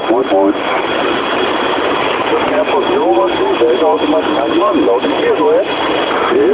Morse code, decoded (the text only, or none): none